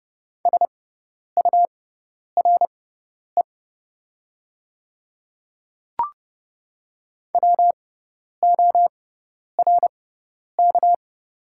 Morse code, heard as a keyed tone, about 30 wpm, spaced as for 10 wpm, 700 Hz. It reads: SURE WORK
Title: 6x Character Spacing - Top 300 Words - 30wpm